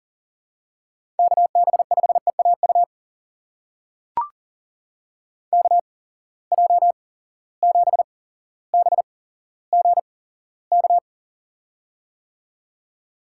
Morse code, heard as a keyed tone, about 41 wpm, spaced as for 17 wpm, 700 Hz. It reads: KB5EAU KJ7BGK
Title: USA Call Signs - 6 Characters - 8x Character Spacing - 40wpm